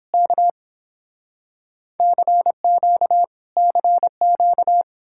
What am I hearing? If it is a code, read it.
Morse code, heard as a keyed tone, about 30 wpm, 700 Hz. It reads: K CQ CQ